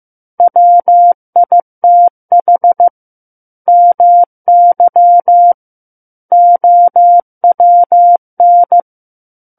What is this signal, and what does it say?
Morse code, heard as a keyed tone, about 15 wpm, 700 Hz. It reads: WITH MY OWN
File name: Chapter 6 - 15 WPM